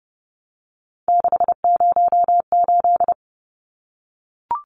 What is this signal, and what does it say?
Morse code, heard as a keyed tone, about 30 wpm, 700 Hz. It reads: 608
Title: Numbers - 3-Digit 30wpm